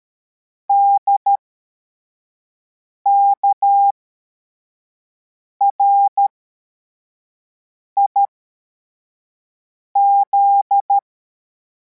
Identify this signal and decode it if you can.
Morse code, heard as a keyed tone, about 13 wpm, spaced as for 4 wpm, 800 Hz. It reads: DKRIZ